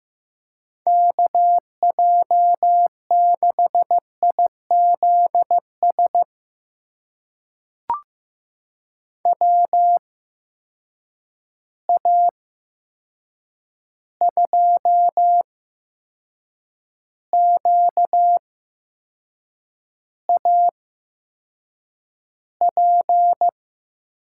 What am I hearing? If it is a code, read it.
Morse code, heard as a keyed tone, about 15 wpm, spaced as for 6 wpm, 700 Hz. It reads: KJ6IZS WA2QAP
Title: USA Call Signs - 6 Characters - 8x Character Spacing - 15wpm